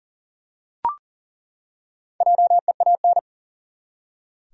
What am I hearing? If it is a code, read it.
Morse code, heard as a keyed tone, about 40 wpm, 700 Hz. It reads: JEAN
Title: USA Common Names - Max 4 Letters - 40wpm